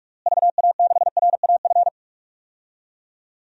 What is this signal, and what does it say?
Morse code, heard as a keyed tone, about 46 wpm, 700 Hz. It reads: VA6RAF
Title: DXCC-100 Call Signs - 6 Characters - Repeated 2x - 45wpm